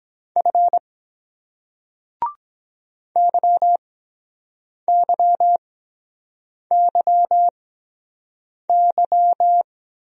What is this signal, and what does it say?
Morse code, heard as a keyed tone, about 22 wpm, spaced as for 7 wpm, 700 Hz. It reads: F YYYY